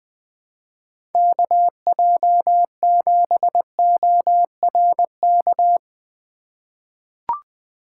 Morse code, heard as a keyed tone, about 20 wpm, 700 Hz. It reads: KJ7ORK
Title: 2023 POTA Activator Call Signs - 20wpm